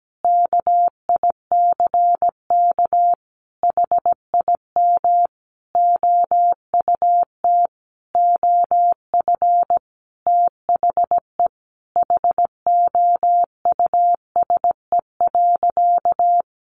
Morse code, heard as a keyed tone, about 17 wpm, 700 Hz. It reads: KICK HIM OUT OF THE HOUSE.
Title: Sentences from Top 1300 Words 17wpm